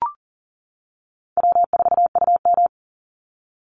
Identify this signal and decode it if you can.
Morse code, heard as a keyed tone, about 39 wpm, 700 Hz. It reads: W4UM
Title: Easy Call Signs - 40wpm